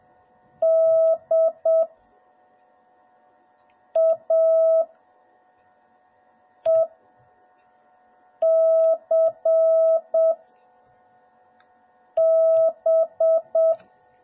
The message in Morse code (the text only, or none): DAECB